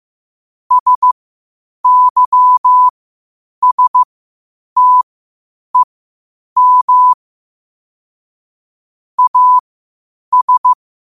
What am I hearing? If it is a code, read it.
Morse code, heard as a keyed tone, about 15 wpm, spaced as for 8 wpm, 1000 Hz. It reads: SYSTEM AS
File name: qotdfiles / 8wpm